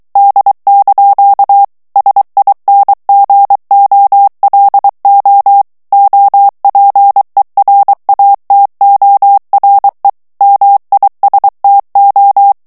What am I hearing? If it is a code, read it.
Morse code, heard as a keyed tone, about 23 wpm, 800 Hz. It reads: D) SINGOLO OPERATORE MISTO
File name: ITALIEN texte clair 2 vitesse 22 mots minute